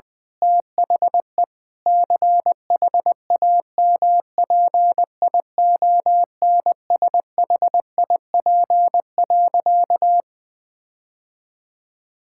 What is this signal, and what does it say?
Morse code, heard as a keyed tone, about 20 wpm, 700 Hz. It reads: THE CHAMPIONSHIP.
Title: Sentences with 8-Word Phrases - 20wpm